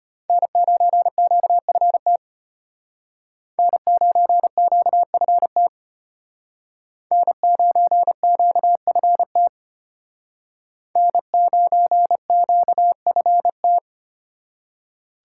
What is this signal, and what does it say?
Morse code, heard as a keyed tone, about 31 wpm, 700 Hz. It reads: N9QFT N9QFT N9QFT N9QFT